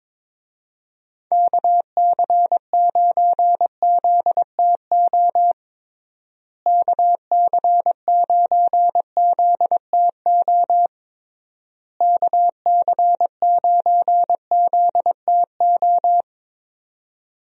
Morse code, heard as a keyed tone, about 22 wpm, 700 Hz. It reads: KC9ZTO KC9ZTO KC9ZTO
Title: USA Call Signs - 6 Characters - Repeated 3x - 22wpm